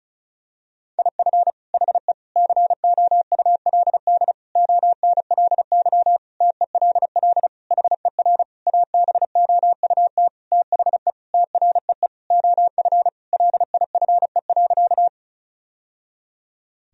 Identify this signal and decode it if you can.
Morse code, heard as a keyed tone, about 34 wpm, 700 Hz. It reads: IF HE COULD ONLY TELL HER ABOUT THE TREE OF LIFE.
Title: Sentences from Top 1000 Words - Review Part 1 - 35wpm